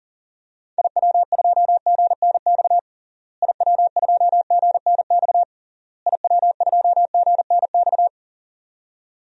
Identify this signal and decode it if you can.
Morse code, heard as a keyed tone, about 40 wpm, 700 Hz. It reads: IW2GNX IW2GNX IW2GNX
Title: DXCC-100 Call Signs - 6 Characters - Repeated 3x - 40wpm